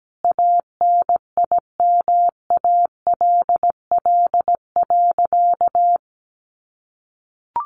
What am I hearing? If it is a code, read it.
Morse code, heard as a keyed tone, about 17 wpm, 700 Hz. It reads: ANIMALL.
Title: A Sound Of Thunder 17wpm